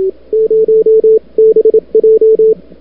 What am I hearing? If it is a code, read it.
Morse code, heard as a keyed tone, about 34 wpm, spaced as for 29 wpm, 440 Hz. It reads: 0 BJ
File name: n6tr / Fab2007CWSprintHour1